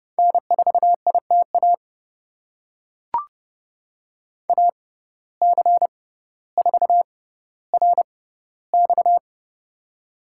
Morse code, heard as a keyed tone, about 30 wpm, spaced as for 15 wpm, 700 Hz. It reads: N4ITA AC4RX
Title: USA Call Signs - 5 Characters - 6x Character Spacing - 30wpm